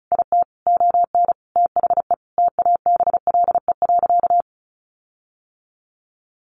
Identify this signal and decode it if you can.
Morse code, heard as a keyed tone, about 34 wpm, 700 Hz. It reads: IT ON THE TABLE.
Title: Sentences from Top 1500 Words 35wpm